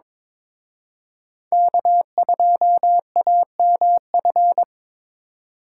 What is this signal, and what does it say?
Morse code, heard as a keyed tone, about 26 wpm, 700 Hz. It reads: K2AMF
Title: Speed Racing - USA 1x3 Call Signs 22wpm